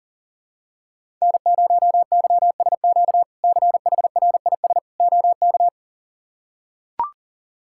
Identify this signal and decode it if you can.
Morse code, heard as a keyed tone, about 40 wpm, 700 Hz. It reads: N0YSQ CHRIS OK
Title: K1USN SST QSO Practice - 40wpm